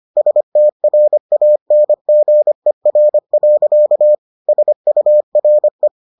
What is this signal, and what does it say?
Morse code, heard as a keyed tone, about 25 wpm, 600 Hz. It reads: STRANGER. SURE